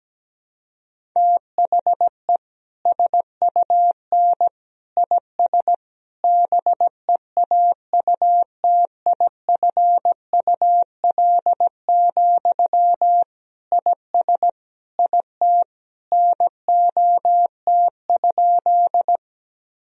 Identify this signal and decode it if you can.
Morse code, heard as a keyed tone, about 17 wpm, 700 Hz. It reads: THE SUN IS BEAUTIFUL, IS IT NOT?
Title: Sentences from Top 1000 Words - Review Part 1 - 17wpm